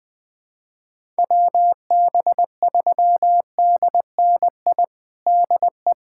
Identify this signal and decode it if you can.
Morse code, heard as a keyed tone, about 21 wpm, 700 Hz. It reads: WB3DNI DE
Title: USA Callsign DE Callsign - 6 Characters - 20wpm